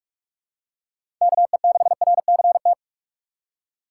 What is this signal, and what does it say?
Morse code, heard as a keyed tone, about 41 wpm, 700 Hz. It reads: KE6RCT
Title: USA Call Signs - 6 Characters - 2x Character Spacing - 45wpm